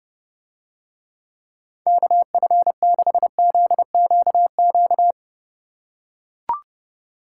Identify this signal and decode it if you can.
Morse code, heard as a keyed tone, about 30 wpm, 700 Hz. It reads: KF6ZQQ